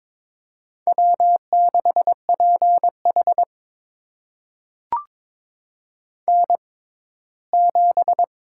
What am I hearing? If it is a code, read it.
Morse code, heard as a keyed tone, about 22 wpm, spaced as for 15 wpm, 700 Hz. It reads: W6PH N7
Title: USA Call Signs - 4 Characters - 6x Character Spacing - 22wpm